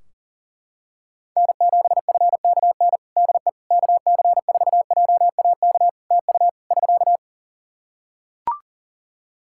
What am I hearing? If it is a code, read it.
Morse code, heard as a keyed tone, about 40 wpm, 700 Hz. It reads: N7FKN DE KC4JAK TU <SK>
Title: Practice QSO - Exchange 4 - Ending - 40wpm